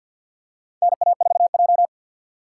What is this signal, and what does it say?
Morse code, heard as a keyed tone, about 52 wpm, 700 Hz. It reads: NA4J
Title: USA Call Signs - 3x Character Spacing - 50wpm